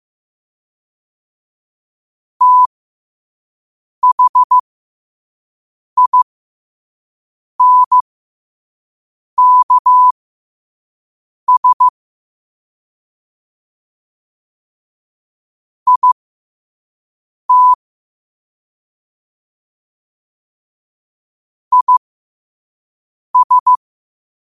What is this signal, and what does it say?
Morse code, heard as a keyed tone, about 15 wpm, spaced as for 5 wpm, 1000 Hz. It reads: THINKS IT IS
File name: qotdfiles / 5wpm